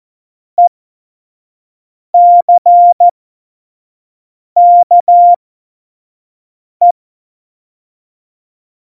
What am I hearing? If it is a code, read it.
Morse code, heard as a keyed tone, about 14 wpm, spaced as for 5 wpm, 700 Hz. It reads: ECKE
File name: Morsekurs / Witze / Witz 25 Juli 17, Tempo 25 F70